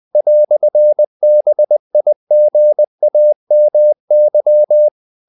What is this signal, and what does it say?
Morse code, heard as a keyed tone, about 20 wpm, 600 Hz. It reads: "BIGAMY
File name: CW / Practice / EnglishQuotations0320